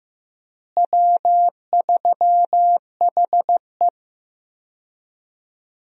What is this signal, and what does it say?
Morse code, heard as a keyed tone, about 15 wpm, 700 Hz. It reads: W3HE
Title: USA Call Signs - 4 Characters - Repeated 3x - 15wpm